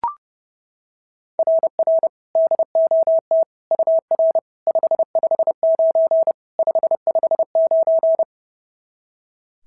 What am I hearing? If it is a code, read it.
Morse code, heard as a keyed tone, about 28 wpm, 650 Hz. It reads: RR DOT UR 559 559
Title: SOTA QSO Practice 30wpm